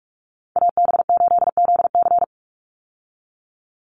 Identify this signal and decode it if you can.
Morse code, heard as a keyed tone, about 44 wpm, 700 Hz. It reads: AB8ZC
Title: USA Call Signs 45wpm